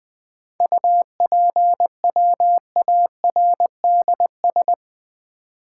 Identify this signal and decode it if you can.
Morse code, heard as a keyed tone, about 20 wpm, 700 Hz. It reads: UPWARDS
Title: 7 Letter Words - First 3 Letters Given - 20wpm